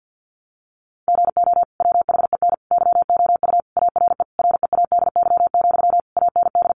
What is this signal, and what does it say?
Morse code, heard as a keyed tone, about 50 wpm, 700 Hz. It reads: GO WHEN YOU ARE READY, AND
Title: Sentences from Top 500 Words 50wpm